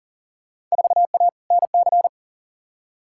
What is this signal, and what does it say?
Morse code, heard as a keyed tone, about 40 wpm, 700 Hz. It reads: 4A NC
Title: Field Day Exchanges - 40wpm